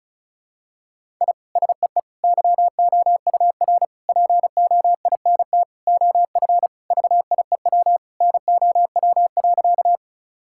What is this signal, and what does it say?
Morse code, heard as a keyed tone, about 34 wpm, 700 Hz. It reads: I SEE YOUR POINT OF VIEW NOW.
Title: Sentences from Top 1000 Words - Review Part 6 - 35wpm